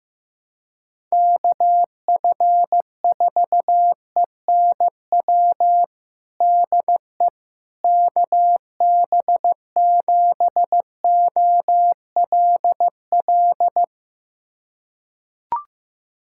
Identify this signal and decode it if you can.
Morse code, heard as a keyed tone, about 15 wpm, 700 Hz. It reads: KF4ENW DE KB7OLL
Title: USA Callsign DE Callsign - 6 Characters - 15wpm